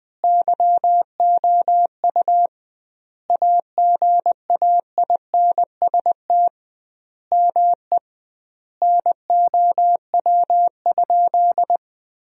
Morse code, heard as a keyed tone, about 20 wpm, 700 Hz. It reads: YOU AGAINST ME NOW?
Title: 2x Word-Spacing - Sentences from Top 400 Words - 20wpm